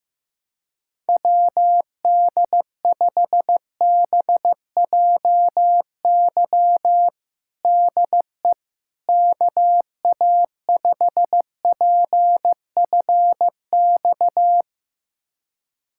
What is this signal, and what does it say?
Morse code, heard as a keyed tone, about 15 wpm, 700 Hz. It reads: WD5BJY DE KA5PFX
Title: USA Callsign DE Callsign - 6 Characters - 15wpm